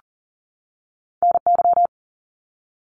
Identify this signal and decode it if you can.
Morse code, heard as a keyed tone, about 39 wpm, 700 Hz. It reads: NY
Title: US States 40wpm